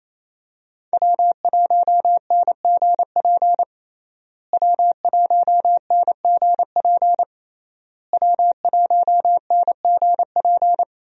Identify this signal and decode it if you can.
Morse code, heard as a keyed tone, about 27 wpm, 700 Hz. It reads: W1NGP W1NGP W1NGP
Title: USA Call Signs - 5 Characters - Repeated 3x - 28wpm